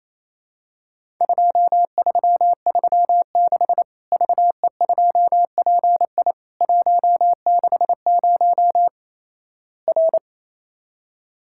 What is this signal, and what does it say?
Morse code, heard as a keyed tone, about 27 wpm, 700 Hz. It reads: 2336 VE2PI 160 R
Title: Worked All Europe DX Contest - QTC Exchange - 28wpm